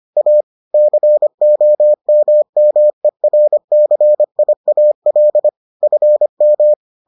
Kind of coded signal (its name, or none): Morse code